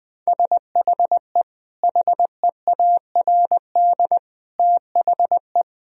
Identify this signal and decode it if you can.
Morse code, heard as a keyed tone, about 21 wpm, 700 Hz. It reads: SHE HEARD THE
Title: Sentences from Top 1000 Words - Review Part 1 - 20wpm